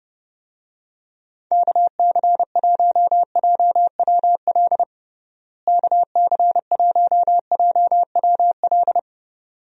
Morse code, heard as a keyed tone, about 30 wpm, 700 Hz. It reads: KC1JWL KC1JWL
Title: USA Call Signs - 6 Characters - Repeated 2x - 30wpm